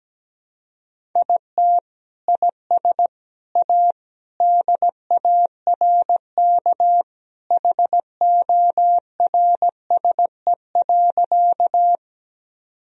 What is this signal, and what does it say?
Morse code, heard as a keyed tone, about 17 wpm, 700 Hz. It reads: IT IS A DARK HORSE.
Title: Sentences from Top 1000 Words - Review Part 1 - 17wpm